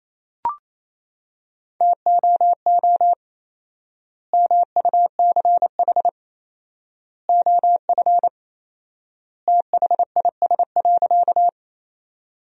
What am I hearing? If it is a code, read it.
Morse code, heard as a keyed tone, about 27 wpm, 700 Hz. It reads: TOO MUCH OF THIS.
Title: 4x Word-Spacing - Sentences from Top 200 Words - 28wpm